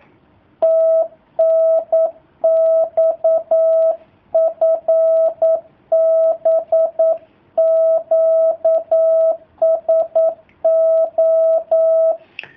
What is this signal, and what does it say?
Morse code, heard as a keyed tone, about 9 wpm, 650 Hz. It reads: TNX FB QSO
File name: jh8cbh / 2019cw / sp09- w